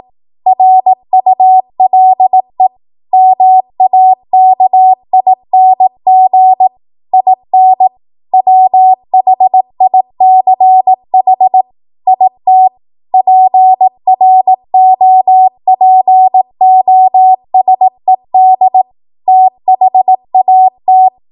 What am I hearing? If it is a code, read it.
Morse code, heard as a keyed tone, about 18 wpm, 750 Hz. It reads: RULE MAKING IN WHICH IT PROPOSED THAT
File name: Morse / 18WPM